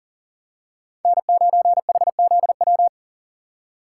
Morse code, heard as a keyed tone, about 40 wpm, 700 Hz. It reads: N9HZW
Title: USA Call Signs - 5 Characters - 2x Character Spacing - 40wpm